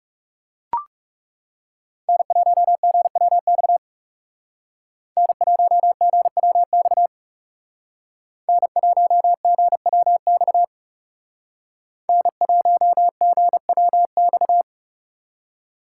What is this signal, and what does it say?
Morse code, heard as a keyed tone, about 37 wpm, 700 Hz. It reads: N1GWX N1GWX N1GWX N1GWX
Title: Speed Racing - USA 1x3 Call Signs 30wpm